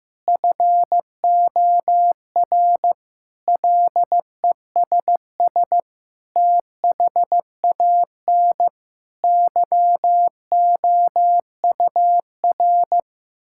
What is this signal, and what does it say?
Morse code, heard as a keyed tone, about 16 wpm, 700 Hz. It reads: FOR LESS THAN YOUR
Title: Sentences from Top 1000 Words - Review Part 5 - 15wpm